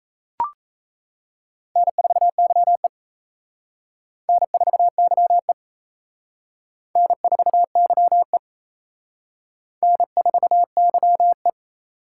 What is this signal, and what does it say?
Morse code, heard as a keyed tone, about 34 wpm, 700 Hz. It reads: N4YE N4YE N4YE N4YE
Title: Speed Racing - USA 1x2 Call Signs 28wpm